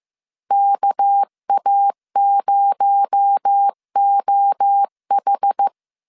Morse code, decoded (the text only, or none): KA0OH